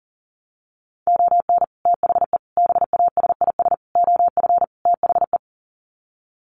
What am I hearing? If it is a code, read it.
Morse code, heard as a keyed tone, about 40 wpm, 700 Hz. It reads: ON THE BASIS OF THE
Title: Sets of 5 Words 40wpm